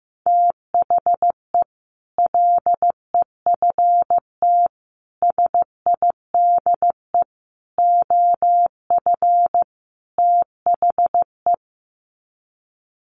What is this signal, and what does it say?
Morse code, heard as a keyed tone, about 15 wpm, 700 Hz. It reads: THE LEFT SIDE OF THE
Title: Sets of 5 Words Encore 15wpm